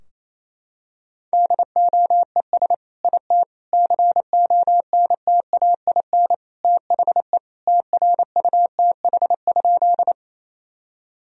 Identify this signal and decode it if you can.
Morse code, heard as a keyed tone, about 27 wpm, 700 Hz. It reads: DOES IT CONTAIN THE TRUTH?
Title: Sentences from Top 1000 Words - Review Part 2 - 28wpm